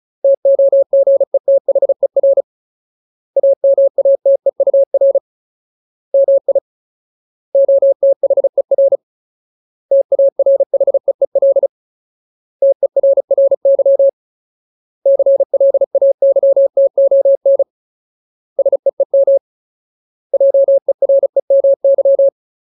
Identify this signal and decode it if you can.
Morse code, heard as a keyed tone, about 35 wpm, 550 Hz. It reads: TOGETHER AMATEUR MI OTHER TARHEEL TERRY CLAYTON SEEM JEREMY
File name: Ultimate Word List / 35WPM Ultimate